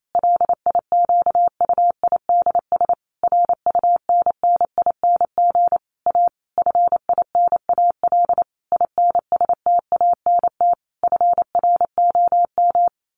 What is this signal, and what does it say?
Morse code, heard as a keyed tone, about 28 wpm, 700 Hz. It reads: LIQUIDS RUNNING A FINAL INSTANT FROM
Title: A Sound Of Thunder 28wpm